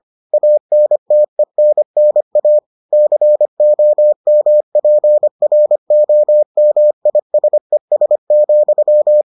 Morse code, heard as a keyed tone, about 25 wpm, 600 Hz. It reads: ANTENNA COMPROMISES,